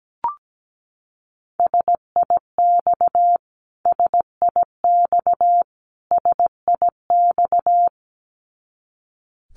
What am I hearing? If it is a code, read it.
Morse code, heard as a keyed tone, about 17 wpm, 700 Hz. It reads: SIX SIX SIX
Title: Top 500 Words Repeated 17wpm